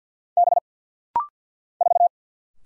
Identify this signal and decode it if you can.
Morse code, heard as a keyed tone, about 50 wpm, 700 Hz. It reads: B 4
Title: Single Letter-Number Alt - Rapid-Fire 50wpm